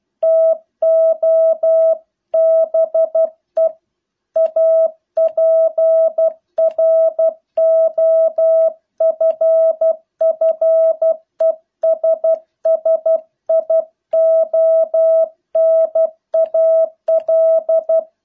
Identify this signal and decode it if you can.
Morse code, heard as a keyed tone, about 12 wpm, 650 Hz. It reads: TO BE A PROFFESSIONAL